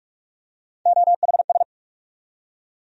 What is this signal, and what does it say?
Morse code, heard as a keyed tone, about 46 wpm, 700 Hz. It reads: OHS
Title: USA Call Sign Postfixes - 3 Letters 45wpm